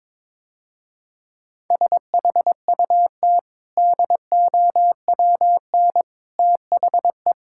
Morse code, heard as a keyed tone, about 22 wpm, 690 Hz. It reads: SHUT DOWN THE
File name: practice / 22wpm / Easy Sentences with Hints - 22wpm